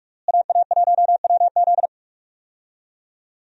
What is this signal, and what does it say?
Morse code, heard as a keyed tone, about 46 wpm, 700 Hz. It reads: AA1WZ
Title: USA Call Signs - 5 Characters - Repeated 2x - 45wpm